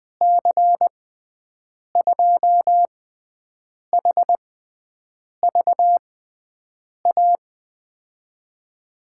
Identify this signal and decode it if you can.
Morse code, heard as a keyed tone, about 21 wpm, spaced as for 7 wpm, 700 Hz. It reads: C2HVA
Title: USA Call Signs - 6 Characters - 6x Character Spacing - 20wpm